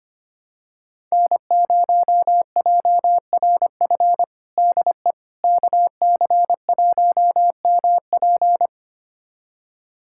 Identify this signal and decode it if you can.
Morse code, heard as a keyed tone, about 25 wpm, 700 Hz. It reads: N0JRF DE KC1MP
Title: USA Callsign DE Callsign - 5 Characters - 25wpm